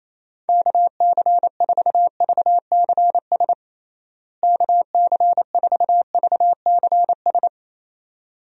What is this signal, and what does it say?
Morse code, heard as a keyed tone, about 28 wpm, 700 Hz. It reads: KC4VCS KC4VCS
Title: USA Call Signs - 6 Characters - Repeated 2x - 28wpm